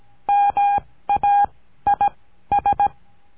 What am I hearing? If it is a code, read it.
Morse code, heard as a keyed tone, about 17 wpm, spaced as for 13 wpm, 810 Hz. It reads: MAIS
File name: CURSO DE TELEGRAFIA